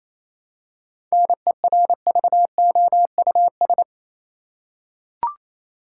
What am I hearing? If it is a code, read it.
Morse code, heard as a keyed tone, about 27 wpm, 700 Hz. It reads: NERVOUS
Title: Seven Letter Words - 28wpm